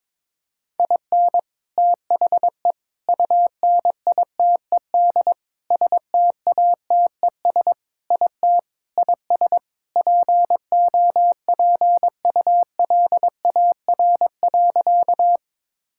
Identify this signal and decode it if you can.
Morse code, heard as a keyed tone, about 22 wpm, 700 Hz. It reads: IN THE UNITED STATES IT IS POPULAR.
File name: practice / 22wpm / Sentences with 3-Word Phrases - 22wpm